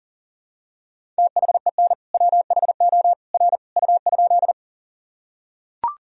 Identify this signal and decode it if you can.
Morse code, heard as a keyed tone, about 40 wpm, 700 Hz. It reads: THEN WHO R U?
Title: Sentences from Top 100 Words and Abbreviations 40wpm